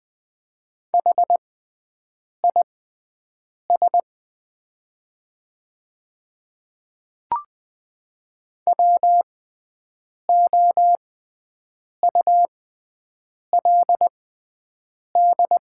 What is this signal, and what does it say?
Morse code, heard as a keyed tone, about 20 wpm, spaced as for 7 wpm, 700 Hz. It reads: HIS WOULD
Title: 6x Character Spacing - Top 200 Words - 20wpm